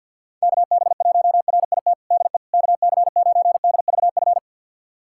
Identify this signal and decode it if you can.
Morse code, heard as a keyed tone, about 46 wpm, 700 Hz. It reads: KB1RIT DE KC9DVF
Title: USA Callsign DE Callsign - 6 Characters - 50wpm